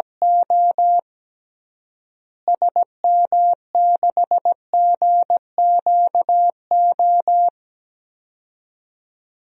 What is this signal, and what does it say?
Morse code, heard as a keyed tone, about 17 wpm, 700 Hz. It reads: O SM6GQO
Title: DXCC-100 Call Signs - 6 Characters - Repeated 2x - 17wpm